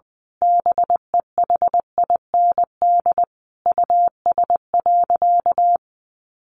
Morse code, heard as a keyed tone, about 20 wpm, 700 Hz. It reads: BEHIND US.
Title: Sentences from Top 1300 Words 20wpm